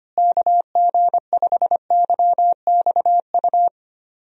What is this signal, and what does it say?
Morse code, heard as a keyed tone, about 25 wpm, 700 Hz. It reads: KG5YXU
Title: USA Call Signs - 6 Characters - Repeated 3x - 25wpm